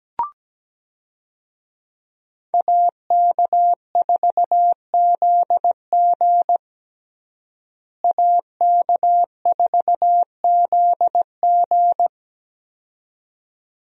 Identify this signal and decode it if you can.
Morse code, heard as a keyed tone, about 17 wpm, 700 Hz. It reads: AK4ZG AK4ZG
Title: USA Call Signs - 5 Characters - Repeated 2x - 17wpm